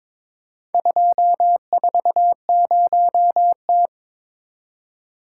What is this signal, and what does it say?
Morse code, heard as a keyed tone, about 21 wpm, 700 Hz. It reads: 240T
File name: practice / 22wpm / Four Letter-Number Combinations - 22wpm